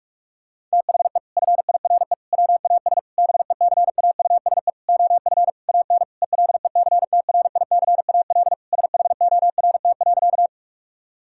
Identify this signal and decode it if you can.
Morse code, heard as a keyed tone, about 41 wpm, 700 Hz. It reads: THE FIRE WAS BECAUSE OF AN ELECTRICAL SHORT.
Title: Sentences from Top 1000 Words - Review Part 6 - 45wpm